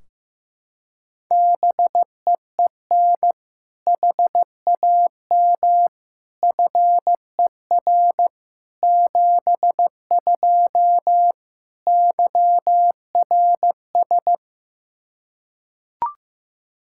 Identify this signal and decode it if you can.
Morse code, heard as a keyed tone, about 15 wpm, 700 Hz. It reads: BEEN HAM FER 72 YRS
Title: Practice QSO - Exchange 3 - More Info - 15wpm